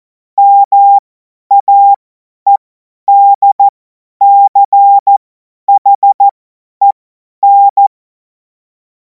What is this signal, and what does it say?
Morse code, heard as a keyed tone, about 14 wpm, spaced as for 10 wpm, 800 Hz. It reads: MAEDCHEN
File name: Morsekurs / Witze / Witz 09. Juni 2026 50 F70